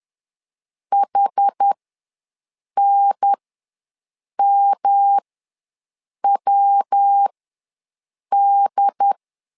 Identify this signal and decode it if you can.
Morse code, heard as a keyed tone, about 10 wpm, spaced as for 6 wpm, 800 Hz. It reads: HNMWD
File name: LEZIONI / 3 MWRO / 3 MWRO ws 10wpm 2310